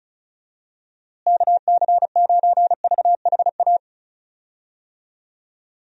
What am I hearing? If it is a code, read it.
Morse code, heard as a keyed tone, about 34 wpm, 700 Hz. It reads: KC9VHA
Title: USA Call Signs - 6 Characters - Repeated 3x - 35wpm